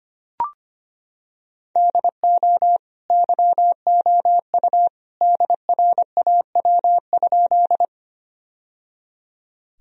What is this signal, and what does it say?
Morse code, heard as a keyed tone, about 25 wpm, 700 Hz. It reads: DO YOU DRAW?
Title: Sentences from Top 1000 Words - Review Part 1 - 25wpm